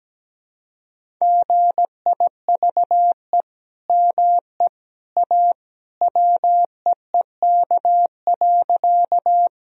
Morse code, heard as a keyed tone, about 17 wpm, 700 Hz. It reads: GIVE ME A WEEK.
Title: Sentences from Top 1000 Words - Review Part 2 - 17wpm